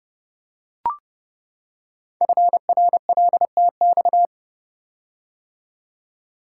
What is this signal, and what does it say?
Morse code, heard as a keyed tone, about 30 wpm, 700 Hz. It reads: FRLTX
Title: Code Groups - 5 Letters - 30wpm